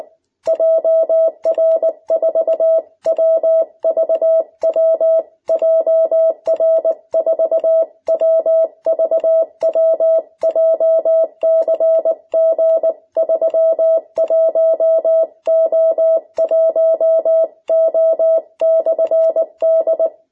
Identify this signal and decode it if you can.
Morse code, heard as a keyed tone, about 19 wpm, 650 Hz. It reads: JR4WVW JR4WVW JCG31O1O/D
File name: jh8cbh / 2021cw